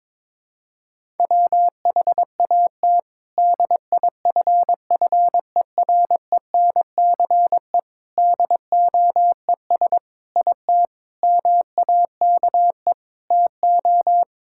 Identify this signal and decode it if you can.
Morse code, heard as a keyed tone, about 22 wpm, 700 Hz. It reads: WHAT DIFFERENCE DOES IT MAKE TO
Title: Sentences from Top 1000 Words - Review Part 5 - 22wpm